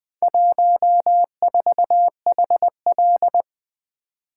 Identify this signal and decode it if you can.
Morse code, heard as a keyed tone, about 20 wpm, 700 Hz. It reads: 14HL
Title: Four Letter-Number Combinations - 20wpm